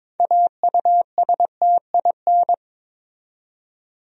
Morse code, heard as a keyed tone, about 22 wpm, 700 Hz. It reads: AUSTIN
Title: USA Common Names - 22wpm